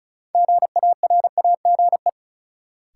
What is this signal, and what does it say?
Morse code, heard as a keyed tone, about 36 wpm, 700 Hz. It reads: GARAGE
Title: Sentences from Top 1600 Words - New Word Drill Alt - 35wpm